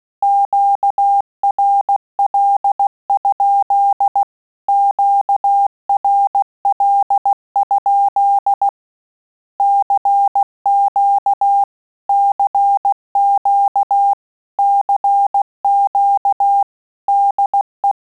Morse code, heard as a keyed tone, about 16 wpm, 800 Hz. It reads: QRL? QRL? CQ CQ CQ DE
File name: QSO CW 75 CPM